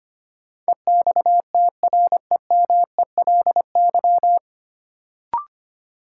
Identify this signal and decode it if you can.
Morse code, heard as a keyed tone, about 25 wpm, 700 Hz. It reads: EXTREMELY
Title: Sentences from Top 1000 Words - Review Part 7 - New Word Drill - 25wpm